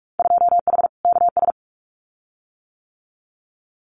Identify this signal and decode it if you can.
Morse code, heard as a keyed tone, about 46 wpm, 700 Hz. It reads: 2H KS
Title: Winter Field Day Exchanges Encore 45wpm